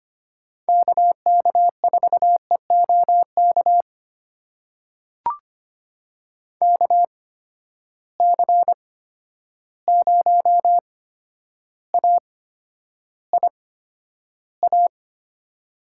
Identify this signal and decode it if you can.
Morse code, heard as a keyed tone, about 25 wpm, spaced as for 11 wpm, 700 Hz. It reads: KK4EOK KC0AIA
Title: USA Call Signs - 6 Characters - 8x Character Spacing - 25wpm